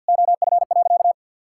Morse code, heard as a keyed tone, about 52 wpm, 700 Hz. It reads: OF.
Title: Sentences from Top 1000 Words - Review Part 5 - 50wpm